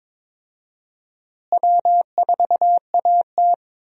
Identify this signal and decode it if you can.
Morse code, heard as a keyed tone, about 22 wpm, 700 Hz. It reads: W4AT
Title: USA Call Signs - 4 Characters - Repeated 2x - 22wpm